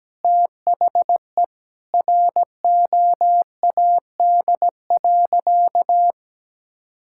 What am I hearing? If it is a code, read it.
Morse code, heard as a keyed tone, about 17 wpm, 700 Hz. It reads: THE ROAD.
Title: Sentences from Top 1600 Words 17wpm